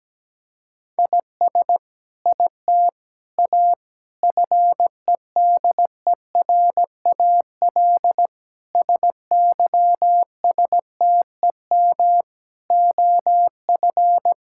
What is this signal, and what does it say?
Morse code, heard as a keyed tone, about 17 wpm, 700 Hz. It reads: IS IT A FEDERAL SYSTEM OF
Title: Sentences from Top 1000 Words - Review Part 5 - 17wpm